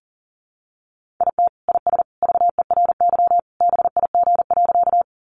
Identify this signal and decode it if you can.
Morse code, heard as a keyed tone, about 40 wpm, 700 Hz. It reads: IT IS VERY BIG.
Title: Easy Phrases 40wpm